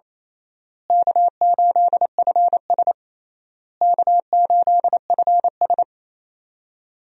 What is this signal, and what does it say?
Morse code, heard as a keyed tone, about 28 wpm, 700 Hz. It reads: K8FS K8FS
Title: USA Call Signs - 4 Characters - Repeated 3x - 28wpm